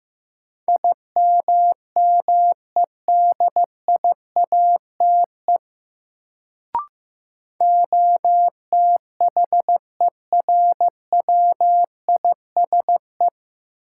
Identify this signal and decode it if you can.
Morse code, heard as a keyed tone, about 16 wpm, 700 Hz. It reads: IMMEDIATE OTHERWISE
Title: Sentences from Top 1200 Words - New Word Drill Alt - 15wpm